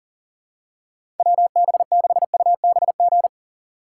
Morse code, heard as a keyed tone, about 40 wpm, 700 Hz. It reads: WB6UBG
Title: USA Call Signs - 6 Characters - 3x Character Spacing - 40wpm